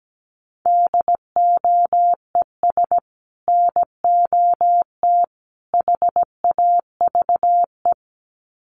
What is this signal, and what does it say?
Morse code, heard as a keyed tone, about 17 wpm, 700 Hz. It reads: DOES NOT HAVE
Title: Sets of 3 Words 17wpm